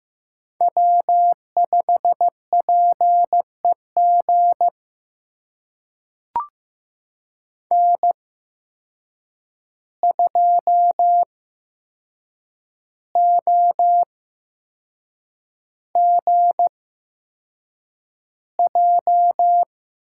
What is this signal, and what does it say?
Morse code, heard as a keyed tone, about 15 wpm, spaced as for 6 wpm, 700 Hz. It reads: W5PEG N2OGJ
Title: USA Call Signs - 5 Characters - 8x Character Spacing - 15wpm